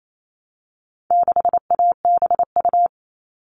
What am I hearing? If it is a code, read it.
Morse code, heard as a keyed tone, about 28 wpm, 700 Hz. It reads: 6ABU
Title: All Cut Numbers - 4-Digit 28wpm